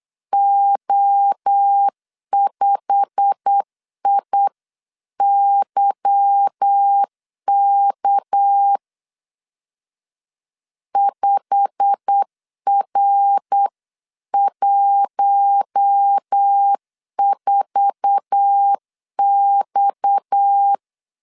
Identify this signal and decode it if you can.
Morse code, heard as a keyed tone, about 8 wpm, 800 Hz. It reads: O5IYK 5R14X
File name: LEZIONI / 8 YJ45 / 8 YJ45 8wpm 3214